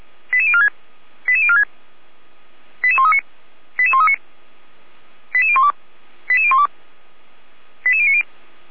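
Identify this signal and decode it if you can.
Five-tone selective call (ZVEI), heard as quick runs of five tones, tones of about 70 ms. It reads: 88046 88046 88139 88139 88012 88012 88090
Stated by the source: ZVEI five-tone selective call received on an SDR, from feuerwehr-orlamuende (fire brigade Orlamünde).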